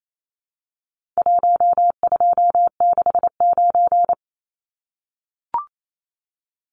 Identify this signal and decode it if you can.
Morse code, heard as a keyed tone, about 28 wpm, 700 Hz. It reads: 1269
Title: Numbers - Contest Style - 4-Digit 28wpm